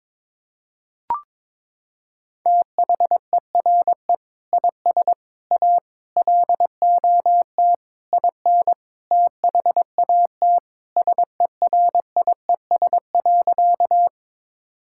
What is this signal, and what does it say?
Morse code, heard as a keyed tone, about 22 wpm, 700 Hz. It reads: THERE IS A LOT IN THAT SERIES.
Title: Sentences from Top 1000 Words - Review Part 5 - 22wpm